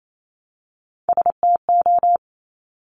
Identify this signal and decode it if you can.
Morse code, heard as a keyed tone, about 28 wpm, 700 Hz. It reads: STO
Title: Top Three Letter Combinations in Words 28wpm